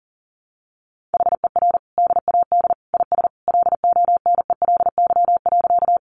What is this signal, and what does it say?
Morse code, heard as a keyed tone, about 40 wpm, 700 Hz. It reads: HER DAD IS LONELY.
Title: Additional Easy Phrases 40wpm